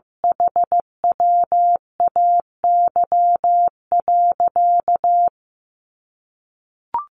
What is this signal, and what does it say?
Morse code, heard as a keyed tone, about 15 wpm, 700 Hz. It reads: HWAY.
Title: Sentences from Top 1400 Words 15wpm